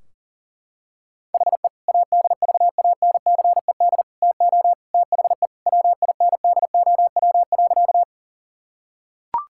Morse code, heard as a keyed tone, about 40 wpm, 700 Hz. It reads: HE ADVANCED TO THE WINDOW.
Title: Sentences from Top 1600 Words 40wpm